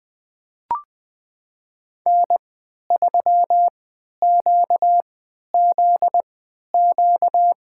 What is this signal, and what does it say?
Morse code, heard as a keyed tone, about 20 wpm, spaced as for 11 wpm, 700 Hz. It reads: N3QZQ
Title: USA Call Signs - 5 Characters - 3x Character Spacing - 20wpm